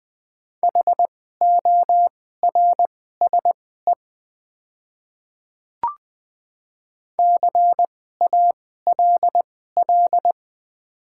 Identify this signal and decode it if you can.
Morse code, heard as a keyed tone, about 20 wpm, spaced as for 14 wpm, 700 Hz. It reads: HORSE CALL
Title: 2x Character Spacing - Top 300 Words - 20wpm